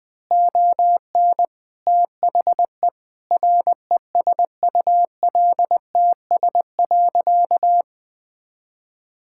Morse code, heard as a keyed tone, about 21 wpm, 700 Hz. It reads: ON THE RESULTS.
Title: Sentences with 7-Word Phrases - 20wpm